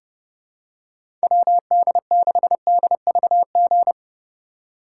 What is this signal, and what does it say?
Morse code, heard as a keyed tone, about 30 wpm, 700 Hz. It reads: WD6DVG
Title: USA Call Signs - 3x Character Spacing - 30wpm